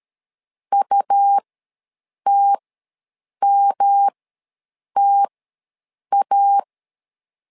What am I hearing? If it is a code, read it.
Morse code, heard as a keyed tone, about 13 wpm, spaced as for 7 wpm, 800 Hz. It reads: UTMTA
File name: LEZIONI / 3 MWRO / 3 M ws 12wpm 1745